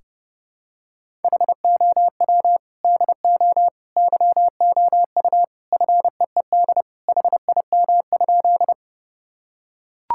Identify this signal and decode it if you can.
Morse code, heard as a keyed tone, about 30 wpm, 700 Hz. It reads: HOW DO YOU FEED HIM?
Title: Sentences from Top 1600 Words 30wpm